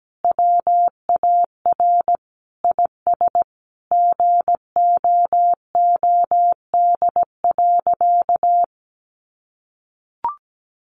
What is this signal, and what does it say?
Morse code, heard as a keyed tone, about 17 wpm, 700 Hz. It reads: WAR IS GOOD.
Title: Sentences from Top 1300 Words 17wpm